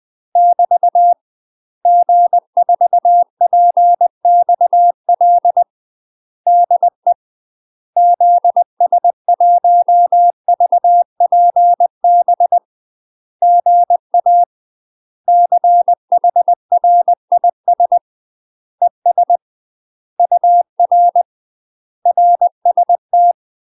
Morse code, heard as a keyed tone, about 20 wpm, 700 Hz. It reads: <BT> G4PXL DE ZS1VPB GA CHRIS ES UR RST
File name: UK MASTER QSOs TRAINING / 20WPM / UK MASTER QSOs 0036